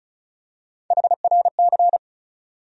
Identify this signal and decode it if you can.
Morse code, heard as a keyed tone, about 36 wpm, 700 Hz. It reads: HRC
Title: USA Call Sign Postfixes - 3 Letters 35wpm